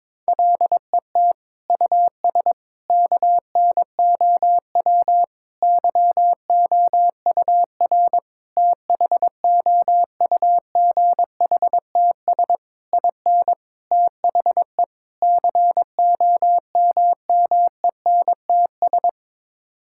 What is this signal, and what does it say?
Morse code, heard as a keyed tone, about 22 wpm, 700 Hz. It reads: LET US KNOW YOUR THOUGHTS IN THE COMMENTS
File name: practice / 22wpm / Sets of 8 Words - 22wpm